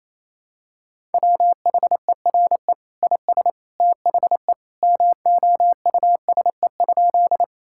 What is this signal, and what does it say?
Morse code, heard as a keyed tone, about 28 wpm, 700 Hz. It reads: WHERE IS THE MOUSE?
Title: Sentences from Top 1000 Words - Review Part 5 - 28wpm